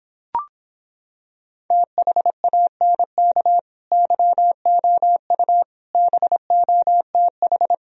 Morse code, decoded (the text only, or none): THANK YOU BOTH